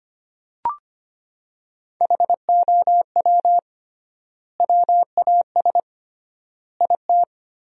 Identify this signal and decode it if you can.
Morse code, heard as a keyed tone, about 25 wpm, 700 Hz. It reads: HOW WAS IT
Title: 3x Word-Spacing - Sentences from Top 500 Words - 25wpm